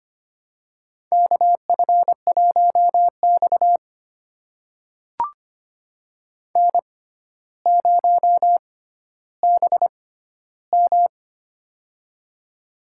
Morse code, heard as a keyed tone, about 25 wpm, spaced as for 13 wpm, 700 Hz. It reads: KF1X N0BM
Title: USA Call Signs - 4 Characters - 6x Character Spacing - 25wpm